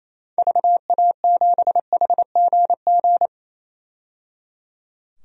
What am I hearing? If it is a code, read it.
Morse code, heard as a keyed tone, about 28 wpm, 700 Hz. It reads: VA7HGG
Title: DXCC-100 Call Signs - 6 Characters - 28wpm